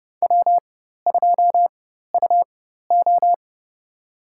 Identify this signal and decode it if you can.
Morse code, heard as a keyed tone, about 30 wpm, spaced as for 14 wpm, 700 Hz. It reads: W2UO
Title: USA Call Signs - 4 Characters - 4x Character Spacing - 30wpm